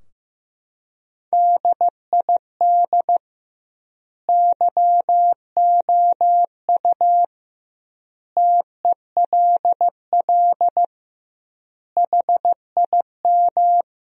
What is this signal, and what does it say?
Morse code, heard as a keyed tone, about 15 wpm, 700 Hz. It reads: DID YOU TELL HIM
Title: 2x Word-Spacing - Sentences from Top 400 Words - 15wpm